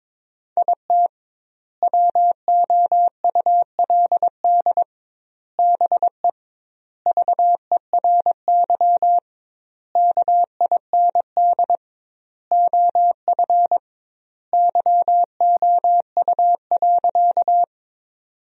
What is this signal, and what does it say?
Morse code, heard as a keyed tone, about 22 wpm, 700 Hz. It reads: IT WOULD BE VERY KIND OF YOU.
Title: 2x Word-Spacing - Sentences from Top 200 Words - 22wpm